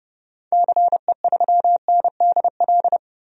Morse code, heard as a keyed tone, about 30 wpm, 700 Hz. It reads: CE3NDL
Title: DXCC-100 Call Signs - 6 Characters - 30wpm